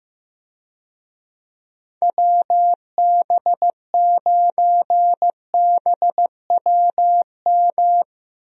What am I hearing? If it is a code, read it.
Morse code, heard as a keyed tone, about 15 wpm, 700 Hz. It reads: WB9BWM
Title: USA Call Signs - 6 Characters - 15wpm